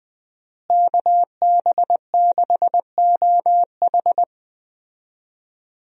Morse code, heard as a keyed tone, about 20 wpm, 700 Hz. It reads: KB6OH
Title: USA Call Signs - 5 Characters - Repeated 2x - 20wpm